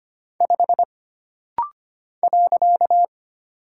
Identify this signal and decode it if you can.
Morse code, heard as a keyed tone, about 25 wpm, 700 Hz. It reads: 5 .
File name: Single Character (Letter:Number:Punctuation:Prosign) - Rapid-Fire - 25wpm